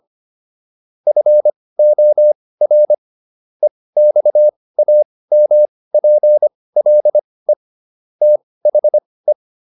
Morse code, heard as a keyed tone, about 25 wpm, spaced as for 18 wpm, 600 Hz. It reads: FOR EXAMPLE THE